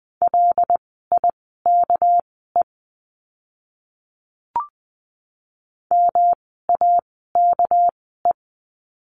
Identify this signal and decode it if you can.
Morse code, heard as a keyed tone, about 20 wpm, spaced as for 15 wpm, 700 Hz. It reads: LIKE MAKE
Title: 237 Lesson 38f - Letter q - Words 20wpm